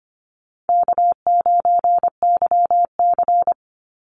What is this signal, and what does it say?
Morse code, heard as a keyed tone, about 25 wpm, 700 Hz. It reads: K9YC
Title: Call Signs Repeated 3x - 25wpm